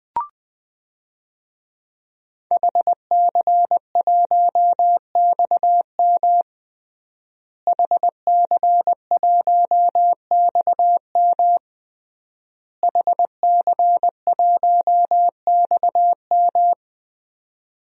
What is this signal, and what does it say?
Morse code, heard as a keyed tone, about 20 wpm, 700 Hz. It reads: HC1XM HC1XM HC1XM
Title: DXCC-100 Call Signs - 5 Characters - Repeated 3x - 20wpm